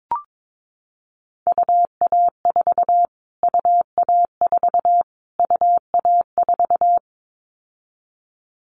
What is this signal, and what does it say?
Morse code, heard as a keyed tone, about 22 wpm, 700 Hz. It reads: UA4 UA4 UA4
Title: Callsign Prefixes for Top 25 DXCC - Repeated 3x - 22wpm